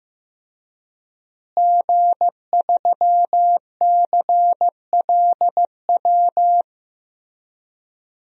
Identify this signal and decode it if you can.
Morse code, heard as a keyed tone, about 15 wpm, 700 Hz. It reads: G3CLW
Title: DXCC-100 Call Signs - 5 Characters - Repeated 3x - 15wpm